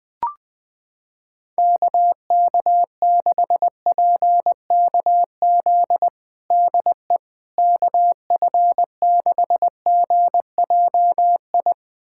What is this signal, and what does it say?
Morse code, heard as a keyed tone, about 20 wpm, 700 Hz. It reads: KK6PKZ DE KF6GJI
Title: USA Callsign DE Callsign 20wpm